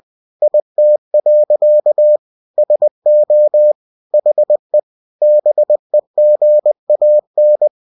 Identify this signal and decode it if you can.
Morse code, heard as a keyed tone, about 20 wpm, 600 Hz. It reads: IT. SO HE BEGAN
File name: cw-books / winnie-the-pooh / winnie-20wpm-page-0002